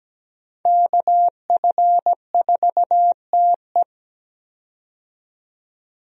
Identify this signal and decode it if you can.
Morse code, heard as a keyed tone, about 17 wpm, 700 Hz. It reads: KF4TE
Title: USA Call Signs - 5 Characters - 17wpm